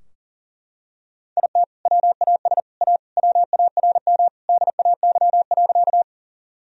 Morse code, heard as a keyed tone, about 40 wpm, 700 Hz. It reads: IT WAS A WARM DAY.
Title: Sentences from Top 1000 Words - Review Part 2 - 40wpm